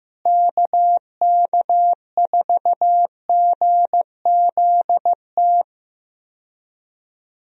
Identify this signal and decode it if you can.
Morse code, heard as a keyed tone, about 15 wpm, 700 Hz. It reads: KK4GZT
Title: USA Call Signs - 6 Characters - Repeated 3x - 15wpm